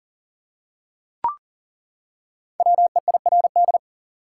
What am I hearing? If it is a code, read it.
Morse code, heard as a keyed tone, about 40 wpm, 700 Hz. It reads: WEIRD
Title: Five Letter Words - 40wpm